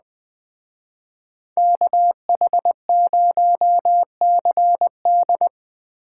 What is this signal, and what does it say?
Morse code, heard as a keyed tone, about 21 wpm, 700 Hz. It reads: KH0CD